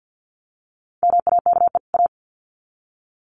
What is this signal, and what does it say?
Morse code, heard as a keyed tone, about 50 wpm, 700 Hz. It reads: MAKE A
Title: Sets of 2 Words 50wpm